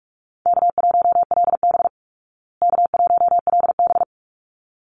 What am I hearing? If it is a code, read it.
Morse code, heard as a keyed tone, about 44 wpm, 700 Hz. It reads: K1LB K1LB
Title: Easy Call Signs Repeated 2x - 45wpm